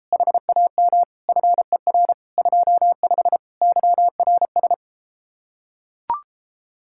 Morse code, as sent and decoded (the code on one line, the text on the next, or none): .... .- -- / ..-. .-. / ..--- ..... / -.-- .-. ...
HAM FER 25 YRS